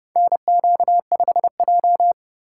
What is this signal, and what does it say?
Morse code, heard as a keyed tone, about 30 wpm, 700 Hz. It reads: NQ5J
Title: USA Call Signs - 4 Characters - 2x Character Spacing - 30wpm